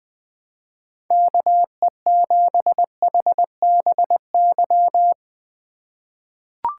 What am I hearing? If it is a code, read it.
Morse code, heard as a keyed tone, about 20 wpm, 700 Hz. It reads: KE7HBY
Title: USA Call Signs - 2x Character Spacing - 20wpm